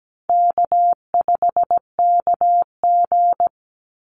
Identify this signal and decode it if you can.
Morse code, heard as a keyed tone, about 17 wpm, 700 Hz. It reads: K5KG
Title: Easy Call Signs Repeated 3x - 17wpm